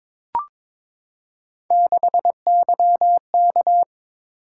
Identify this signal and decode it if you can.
Morse code, heard as a keyed tone, about 22 wpm, 700 Hz. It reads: 6YK
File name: practice / 22wpm / Three Letter-Number Combinations - 22wpm